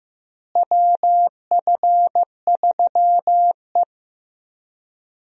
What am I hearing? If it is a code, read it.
Morse code, heard as a keyed tone, about 15 wpm, 700 Hz. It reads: WF3E
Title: USA Call Signs - 4 Characters - Repeated 2x - 15wpm